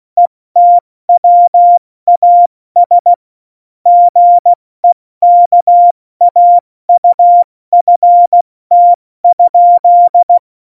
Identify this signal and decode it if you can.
Morse code, heard as a keyed tone, about 16 wpm, 700 Hz. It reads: ETWAS GEKAUFT?
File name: Morsekurs / Witze / Witz 24. November 2020, 70 F80